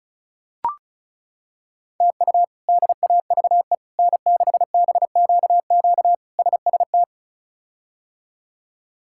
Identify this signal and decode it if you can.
Morse code, heard as a keyed tone, about 34 wpm, 700 Hz. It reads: TU DAVE N6BQQ SST
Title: K1USN SST QSO Practice - 35wpm